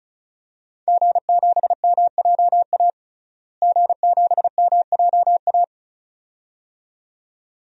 Morse code, heard as a keyed tone, about 34 wpm, 700 Hz. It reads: G7MJA G7MJA
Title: DXCC-100 Call Signs - 5 Characters - Repeated 2x - 35wpm